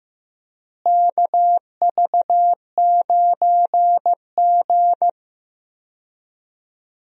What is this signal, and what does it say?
Morse code, heard as a keyed tone, about 15 wpm, 700 Hz. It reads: KV9G
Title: USA Call Signs - 4 Characters - Repeated 3x - 15wpm